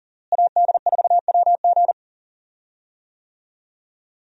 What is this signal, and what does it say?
Morse code, heard as a keyed tone, about 39 wpm, 700 Hz. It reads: AD4WG
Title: USA Call Signs - 5 Characters - Repeated 2x - 40wpm